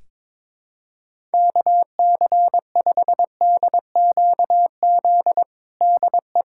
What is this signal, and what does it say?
Morse code, heard as a keyed tone, about 22 wpm, 700 Hz. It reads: KC5DQZ DE
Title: USA Callsign DE Callsign 22wpm